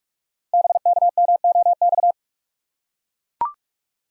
Morse code, heard as a keyed tone, about 44 wpm, 700 Hz. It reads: BKMOX
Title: Code Groups - 5 Letters - 45wpm